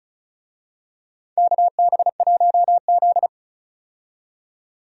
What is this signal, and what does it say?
Morse code, heard as a keyed tone, about 36 wpm, 700 Hz. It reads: KB1Z
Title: USA Call Signs - 4 Characters - 35wpm